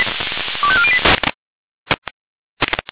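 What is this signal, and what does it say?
Five-tone selective call (ZVEI), heard as quick runs of five tones, tones of about 70 ms. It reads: 25408
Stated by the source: ZVEI five-tone selective call 25897 received by SDR.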